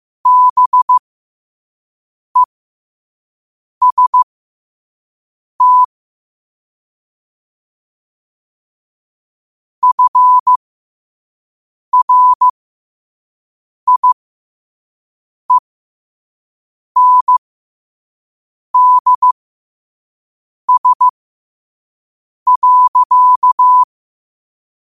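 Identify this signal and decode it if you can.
Morse code, heard as a keyed tone, about 15 wpm, spaced as for 5 wpm, 1000 Hz. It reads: BEST FRIENDS.
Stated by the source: Morse code at 5 wpm.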